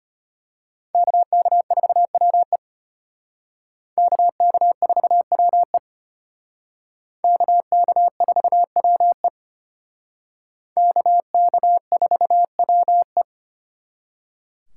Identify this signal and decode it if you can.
Morse code, heard as a keyed tone, about 31 wpm, 700 Hz. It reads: KK4WE KK4WE KK4WE KK4WE
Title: Speed Racing - USA 2x2 Call Signs 25wpm